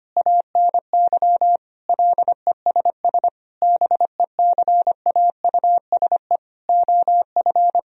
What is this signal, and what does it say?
Morse code, heard as a keyed tone, about 25 wpm, 700 Hz. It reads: ANY LESS BECAUSE OF
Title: Sentences from Top 1000 Words - Review Part 2 - 25wpm